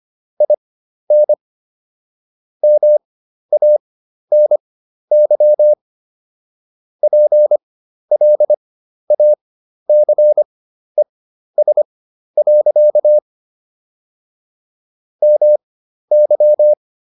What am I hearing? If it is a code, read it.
Morse code, heard as a keyed tone, about 25 wpm, spaced as for 12 wpm, 600 Hz. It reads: IN MANY PLACES. MY